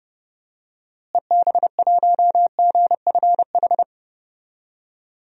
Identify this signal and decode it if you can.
Morse code, heard as a keyed tone, about 30 wpm, 700 Hz. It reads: EB1GFH
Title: DXCC-100 Call Signs - 6 Characters - Repeated 3x - 30wpm